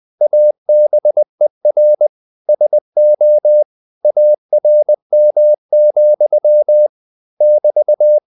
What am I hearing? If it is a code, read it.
Morse code, heard as a keyed tone, about 20 wpm, 600 Hz. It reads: ABER SO ARM, <BT>